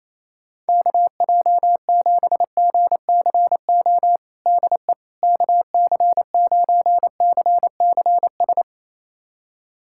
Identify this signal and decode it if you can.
Morse code, heard as a keyed tone, about 28 wpm, 700 Hz. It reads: KJ7GCO DE KC9CCS
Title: USA Callsign DE Callsign 28wpm